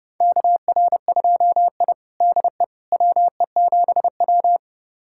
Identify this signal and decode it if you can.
Morse code, heard as a keyed tone, about 30 wpm, 700 Hz. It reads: KR2I DE WE7W
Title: USA Callsign DE Callsign - 4 Characters - 30wpm